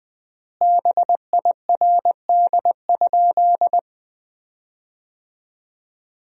Morse code, heard as a keyed tone, about 21 wpm, 700 Hz. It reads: BIRD?
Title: 3x Word-Spacing - Sentences from Top 400 Words - 20wpm